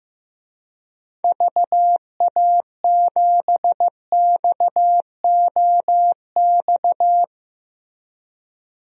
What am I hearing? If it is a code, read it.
Morse code, heard as a keyed tone, about 15 wpm, 700 Hz. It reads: VA7XOX
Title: DXCC-100 Call Signs - 6 Characters - 15wpm